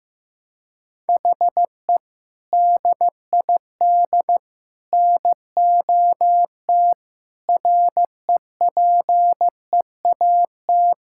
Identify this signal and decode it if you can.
Morse code, heard as a keyed tone, about 15 wpm, 700 Hz. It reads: HE DID NOT REPEAT
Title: Sentences from Top 1600 Words 15wpm